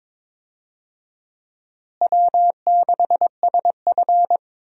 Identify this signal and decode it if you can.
Morse code, heard as a keyed tone, about 22 wpm, 700 Hz. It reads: W6SF
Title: USA Call Signs - 4 Characters - Repeated 3x - 22wpm